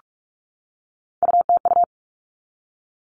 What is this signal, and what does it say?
Morse code, heard as a keyed tone, about 44 wpm, 700 Hz. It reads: UTU
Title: All Cut Numbers - 3-Digit 45wpm